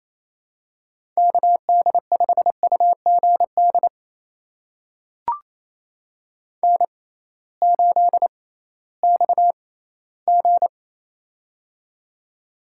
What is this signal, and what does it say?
Morse code, heard as a keyed tone, about 27 wpm, spaced as for 16 wpm, 700 Hz. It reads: KD5UGD N8XG
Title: USA Call Signs - 6x Character Spacing - 28wpm